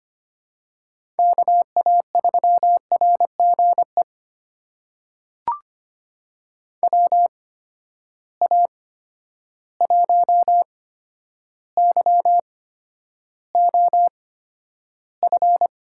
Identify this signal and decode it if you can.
Morse code, heard as a keyed tone, about 25 wpm, spaced as for 11 wpm, 700 Hz. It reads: KA3RGE WA1YOF
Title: USA Call Signs - 6 Characters - 8x Character Spacing - 25wpm